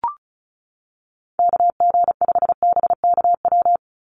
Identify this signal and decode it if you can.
Morse code, heard as a keyed tone, about 34 wpm, 700 Hz. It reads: KG5BKW
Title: USA Call Signs 35wpm